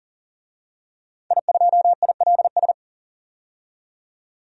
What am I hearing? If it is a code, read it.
Morse code, heard as a keyed tone, about 40 wpm, 700 Hz. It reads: I2ILS
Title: DXCC-100 Call Signs - 5 Characters - 40wpm